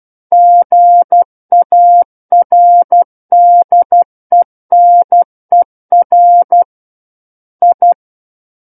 Morse code, heard as a keyed tone, about 12 wpm, 700 Hz. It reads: GARDENER I
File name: Book One / 03-Chapter 3 - 12 WPM